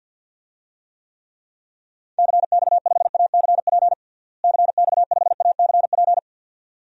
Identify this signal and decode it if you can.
Morse code, heard as a keyed tone, about 51 wpm, 700 Hz. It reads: CX5ACP CX5ACP
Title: DXCC-100 Call Signs - 6 Characters - Repeated 2x - 50wpm